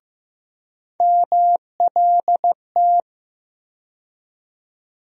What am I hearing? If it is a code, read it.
Morse code, heard as a keyed tone, about 15 wpm, 700 Hz. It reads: MLT
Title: USA Call Sign Postfixes - 3 Letters 15wpm